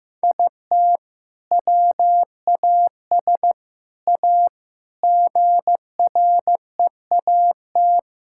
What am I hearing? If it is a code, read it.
Morse code, heard as a keyed tone, about 16 wpm, 700 Hz. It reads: IT WAS A GREAT
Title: Sentences from Top 1000 Words - Review Part 5 - 15wpm